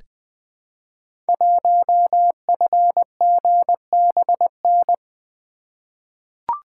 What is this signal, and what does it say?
Morse code, heard as a keyed tone, about 20 wpm, 700 Hz. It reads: 1FGBN
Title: Code Groups - 5 Letter:Numbers - 20wpm